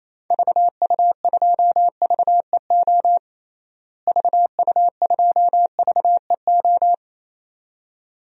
Morse code, heard as a keyed tone, about 28 wpm, 700 Hz. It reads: VU2VEO VU2VEO
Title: DXCC-100 Call Signs - 6 Characters - Repeated 2x - 28wpm